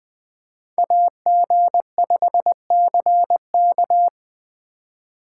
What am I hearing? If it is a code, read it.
Morse code, heard as a keyed tone, about 25 wpm, 700 Hz. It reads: AG5CK
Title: Speed Racing - USA 2x2 Call Signs 20wpm